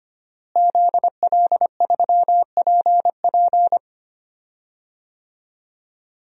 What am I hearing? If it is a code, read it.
Morse code, heard as a keyed tone, about 25 wpm, 700 Hz. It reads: ZL3PP
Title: DXCC-100 Call Signs - 5 Characters - Repeated 2x - 25wpm